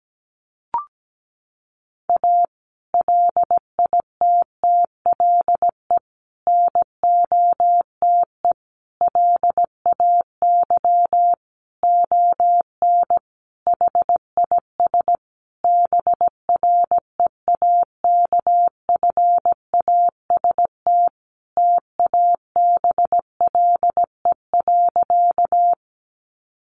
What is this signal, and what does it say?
Morse code, heard as a keyed tone, about 17 wpm, 700 Hz. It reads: A LITTLE NOTE LAY ON HIS BREAKFAST TABLE.
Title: Sentences from Top 1500 Words 17wpm